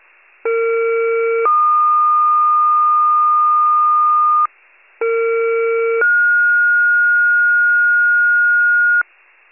Two-tone page, heard as a one-second tone, then a three-second tone, about 460 then 1220 Hz, about 460 then 1430 Hz, with noise. It is heard from 0.4 to 9.0 s.